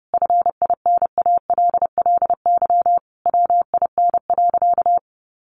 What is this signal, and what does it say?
Morse code, heard as a keyed tone, about 28 wpm, 700 Hz. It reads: FINALLY WIN.
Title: Sentences from Top 1500 Words 30wpm